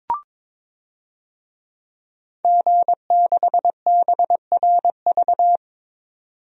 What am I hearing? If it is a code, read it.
Morse code, heard as a keyed tone, about 22 wpm, 700 Hz. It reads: G6BRV
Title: DXCC-100 Call Signs - 5 Characters - 22wpm